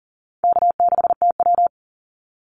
Morse code, heard as a keyed tone, about 40 wpm, 700 Hz. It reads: K6TW
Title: NA SOTA Activator Callsigns 40wpm